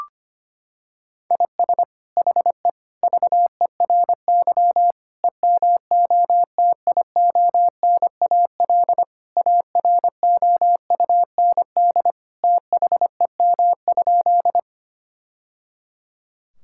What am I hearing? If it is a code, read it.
Morse code, heard as a keyed tone, about 25 wpm, 700 Hz. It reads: IS HE VERY EMOTIONAL AROUND THEM?
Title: Sentences from Top 1000 Words - Review Part 5 - 25wpm